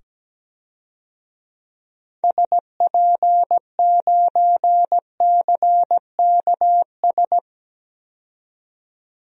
Morse code, heard as a keyed tone, about 17 wpm, 700 Hz. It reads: SP9CKS
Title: DXCC-100 Call Signs - 6 Characters - 17wpm